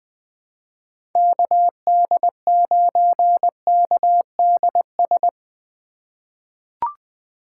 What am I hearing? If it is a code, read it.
Morse code, heard as a keyed tone, about 20 wpm, 700 Hz. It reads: KD9KDS
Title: USA Call Signs - 4x Character Spacing - 20wpm